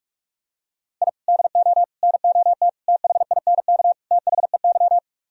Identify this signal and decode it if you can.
Morse code, heard as a keyed tone, about 45 wpm, 700 Hz. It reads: I DO NOT THINK THEY